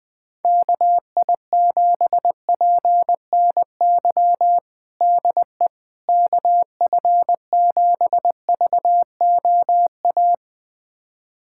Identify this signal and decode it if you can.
Morse code, heard as a keyed tone, about 20 wpm, 700 Hz. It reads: KI7PNY DE KF7VOA
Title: USA Callsign DE Callsign - 6 Characters - 20wpm